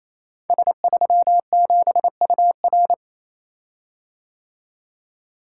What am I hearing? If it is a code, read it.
Morse code, heard as a keyed tone, about 28 wpm, 700 Hz. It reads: S37UR
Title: Code Groups - 5 Letter:Numbers - 28wpm